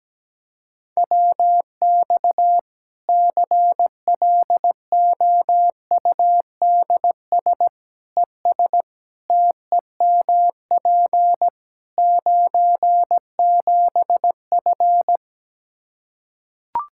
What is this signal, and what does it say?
Morse code, heard as a keyed tone, about 17 wpm, 700 Hz. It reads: WX CLOUDS ES TEMP 97F
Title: Practice QSO - Exchange 2 - Info - 17wpm